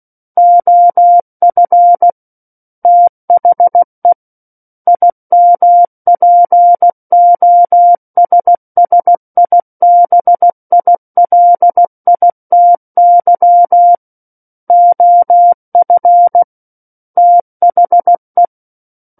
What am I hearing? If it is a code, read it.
Morse code, heard as a keyed tone, about 16 wpm, 700 Hz. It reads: OF THE IMPOSSIBILITY OF THE
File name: Book One / 07-Chapter 7 - 16 WPM